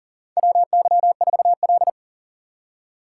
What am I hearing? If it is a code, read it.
Morse code, heard as a keyed tone, about 38 wpm, 700 Hz. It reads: WY4L
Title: USA Call Signs - 4 Characters - 3x Character Spacing - 40wpm